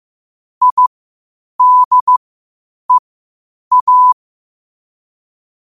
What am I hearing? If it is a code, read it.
Morse code, heard as a keyed tone, about 15 wpm, spaced as for 8 wpm, 1000 Hz. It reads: IDEA
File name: qotdfiles / 8wpm